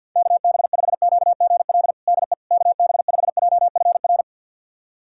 Morse code, heard as a keyed tone, about 53 wpm, 700 Hz. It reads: KB5QGL DE KB5JFR
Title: USA Callsign DE Callsign - 6 Characters - 50wpm